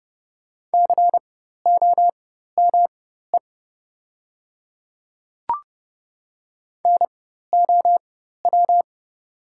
Morse code, heard as a keyed tone, about 30 wpm, spaced as for 14 wpm, 700 Hz. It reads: COME NOW
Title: 4x Character Spacing - Top 100 Words - 30wpm